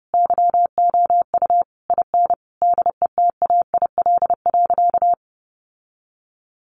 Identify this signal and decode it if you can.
Morse code, heard as a keyed tone, about 30 wpm, 700 Hz. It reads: YOU IN DETAIL.
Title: Sentences from Top 1500 Words 30wpm